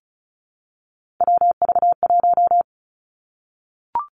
Morse code, heard as a keyed tone, about 36 wpm, 700 Hz. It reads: WV1
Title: Three Challenging Letter-Number Combinations 35wpm